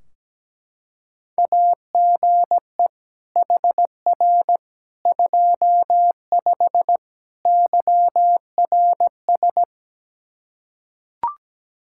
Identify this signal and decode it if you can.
Morse code, heard as a keyed tone, about 17 wpm, 700 Hz. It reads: AGE HR 25 YRS
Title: K1USN SST QSO Practice 17wpm